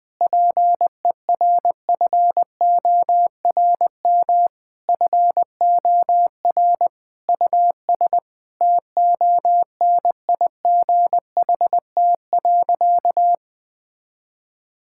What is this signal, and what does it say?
Morse code, heard as a keyed tone, about 21 wpm, 700 Hz. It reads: PERFORM FOR US TONIGHT.
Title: Sentences from Top 1000 Words - Review Part 5 - 20wpm